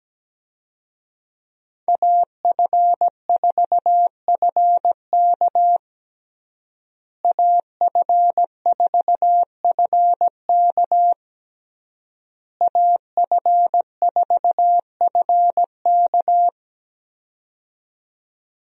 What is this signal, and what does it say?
Morse code, heard as a keyed tone, about 17 wpm, 700 Hz. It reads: AF4FK AF4FK AF4FK
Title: USA Call Signs - 5 Characters - Repeated 3x - 17wpm